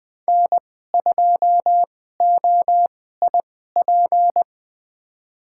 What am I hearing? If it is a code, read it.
Morse code, heard as a keyed tone, about 20 wpm, spaced as for 14 wpm, 700 Hz. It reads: N2OIP
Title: USA Call Signs - 5 Characters - 2x Character Spacing - 20wpm